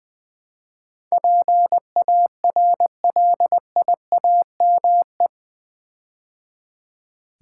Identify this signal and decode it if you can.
Morse code, heard as a keyed tone, about 20 wpm, 700 Hz. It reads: PARLIAME
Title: Build A Word - 4 Letters At A Time - 10-Letter Words - 20wpm